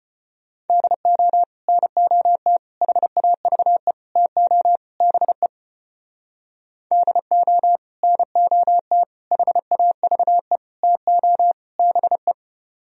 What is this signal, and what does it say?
Morse code, heard as a keyed tone, about 30 wpm, 700 Hz. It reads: DO NOT HAVE TO BE DO NOT HAVE TO BE
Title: Speed-Racing - Sets of 5 Words 25wpm